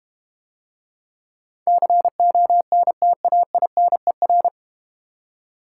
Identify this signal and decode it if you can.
Morse code, heard as a keyed tone, about 31 wpm, 700 Hz. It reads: CONTAINER
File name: Easy Words - Speed-Racing with Hints - 25wpm